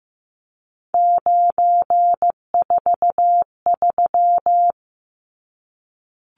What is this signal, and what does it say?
Morse code, heard as a keyed tone, about 15 wpm, 700 Hz. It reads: 943
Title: Numbers - 3-Digit 15wpm